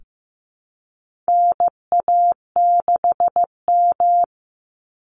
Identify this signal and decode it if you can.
Morse code, heard as a keyed tone, about 15 wpm, 700 Hz. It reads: NA6M
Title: Call Signs Repeated 3x - 15wpm